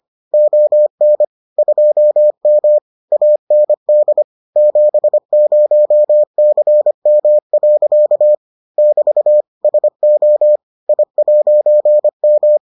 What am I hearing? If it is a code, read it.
Morse code, heard as a keyed tone, about 25 wpm, 600 Hz. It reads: ON 2M AND 70CM. <BT> SO I'M